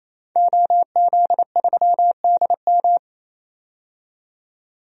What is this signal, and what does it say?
Morse code, heard as a keyed tone, about 27 wpm, 700 Hz. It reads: OZ3DM
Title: DXCC-100 Call Signs - 5 Characters - 28wpm